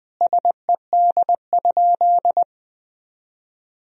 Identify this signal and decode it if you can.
Morse code, heard as a keyed tone, about 21 wpm, 700 Hz. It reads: SED?
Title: Sentences from Top 100 Words and Abbreviations 20wpm